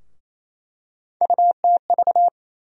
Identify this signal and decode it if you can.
Morse code, heard as a keyed tone, about 28 wpm, 700 Hz. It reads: UTV
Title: USA Call Sign Postfixes - 3 Letters 28wpm